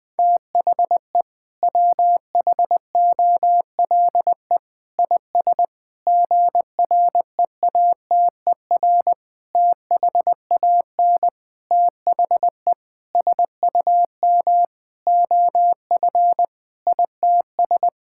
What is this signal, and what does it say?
Morse code, heard as a keyed tone, about 20 wpm, 700 Hz. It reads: THE WHOLE IS GREATER THAN THE SUM OF ITS
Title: Sentences with 6-Word Phrases - Spoken Beginning - 20wpm